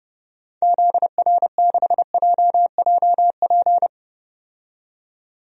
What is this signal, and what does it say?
Morse code, heard as a keyed tone, about 32 wpm, 700 Hz. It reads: ZR6JJP
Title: DXCC-100 Call Signs - 6 Characters - 30wpm